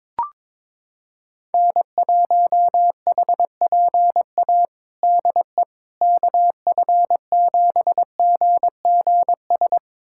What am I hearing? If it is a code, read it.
Morse code, heard as a keyed tone, about 22 wpm, 700 Hz. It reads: N1HPA DE KF7GGS
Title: USA Callsign DE Callsign 22wpm